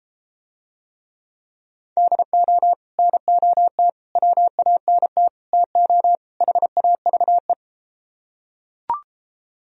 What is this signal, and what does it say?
Morse code, heard as a keyed tone, about 33 wpm, 700 Hz. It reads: DO NOT WANT TO HAVE